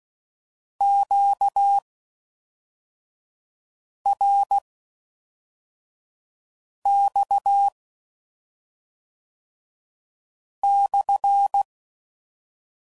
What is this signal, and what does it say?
Morse code, heard as a keyed tone, about 16 wpm, spaced as for 3 wpm, 800 Hz. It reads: QRX/